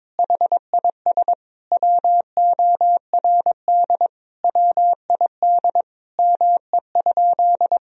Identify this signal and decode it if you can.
Morse code, heard as a keyed tone, about 22 wpm, 700 Hz. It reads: HIS WORD WID ME?
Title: Sentences from Top 300 Words and Abbreviations 22wpm